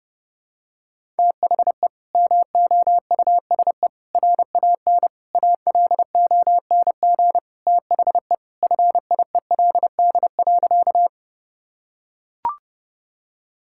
Morse code, heard as a keyed tone, about 32 wpm, 700 Hz. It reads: THE MOUSE RAN ALONG THE FIELD.
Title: Sentences from Top 1000 Words - Review Part 5 - 30wpm